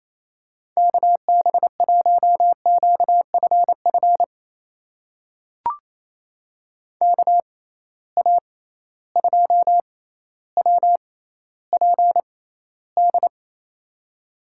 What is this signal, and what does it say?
Morse code, heard as a keyed tone, about 28 wpm, spaced as for 14 wpm, 700 Hz. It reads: KB1QFF KA2WPD
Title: USA Call Signs - 6 Characters - 6x Character Spacing - 28wpm